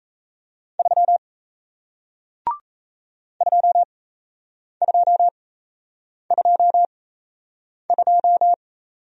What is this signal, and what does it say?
Morse code, heard as a keyed tone, about 38 wpm, spaced as for 8 wpm, 700 Hz. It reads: 3 2222